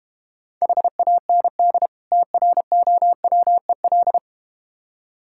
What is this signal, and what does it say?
Morse code, heard as a keyed tone, about 31 wpm, 700 Hz. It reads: HAND TROWEL
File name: practice / 25wpm / Difficult Words - Speed-Racing with Hints - 25wpm